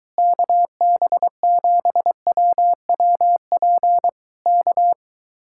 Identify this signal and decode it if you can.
Morse code, heard as a keyed tone, about 22 wpm, 700 Hz. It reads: KB7WWP K